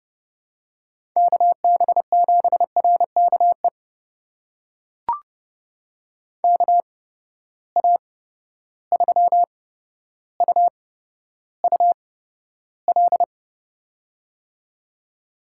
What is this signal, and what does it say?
Morse code, heard as a keyed tone, about 30 wpm, spaced as for 13 wpm, 700 Hz. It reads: KB7RKE KA3UUL
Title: USA Call Signs - 6 Characters - 8x Character Spacing - 30wpm